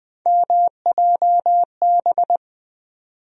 Morse code, heard as a keyed tone, about 20 wpm, 700 Hz. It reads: MJB
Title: USA Call Sign Postfixes - 3 Letters 20wpm